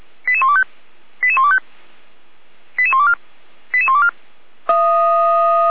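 Five-tone selective call (ZVEI), heard as quick runs of five tones, tones of about 70 ms. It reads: 88125 88125 88124 88124
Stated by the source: ZVEI five-tone selective call 88267, received via SDR.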